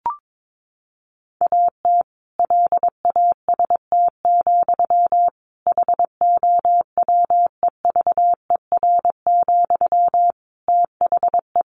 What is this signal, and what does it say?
Morse code, heard as a keyed tone, about 22 wpm, 700 Hz. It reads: AT LAST, HOWEVER, THE
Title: Sentences from Top 1400 Words 22wpm